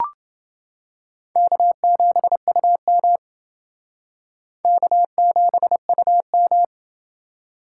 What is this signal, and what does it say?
Morse code, heard as a keyed tone, about 28 wpm, 700 Hz. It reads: K7UM K7UM